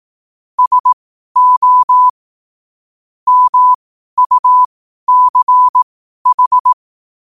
Morse code, heard as a keyed tone, about 18 wpm, spaced as for 12 wpm, 1000 Hz. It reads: SO MUCH
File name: qotdfiles / 12wpm